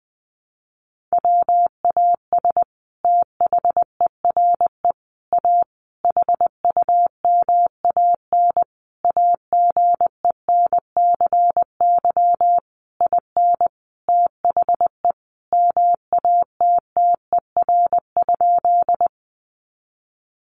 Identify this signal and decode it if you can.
Morse code, heard as a keyed tone, about 20 wpm, 700 Hz. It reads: WAS THERE A HUMAN AGENCY IN THE MATTER?
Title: Sentences from Top 1300 Words 20wpm